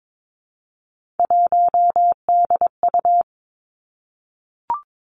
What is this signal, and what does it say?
Morse code, heard as a keyed tone, about 22 wpm, 700 Hz. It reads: 1DU
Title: Three Challenging Letter-Number Combinations 22wpm